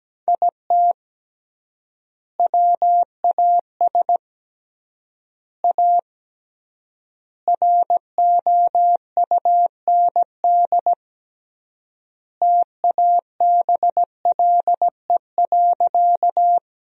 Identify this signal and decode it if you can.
Morse code, heard as a keyed tone, about 17 wpm, 700 Hz. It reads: IT WAS A ROUND TABLE.
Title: 3x Word-Spacing - Sentences from Top 400 Words - 17wpm